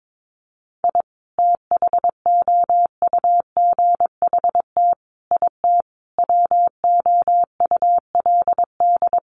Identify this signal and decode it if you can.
Morse code, heard as a keyed tone, about 22 wpm, 700 Hz. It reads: I THOUGHT IT WOULD
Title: Sets of 5 Words Encore 22wpm